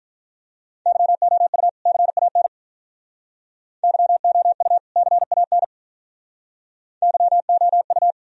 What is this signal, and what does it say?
Morse code, heard as a keyed tone, about 45 wpm, 700 Hz. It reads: YOU CAN YOU CAN YOU